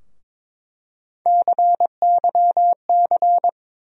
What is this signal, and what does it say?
Morse code, heard as a keyed tone, about 22 wpm, 700 Hz. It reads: CYC
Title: USA Call Sign Postfixes - 3 Letters 22wpm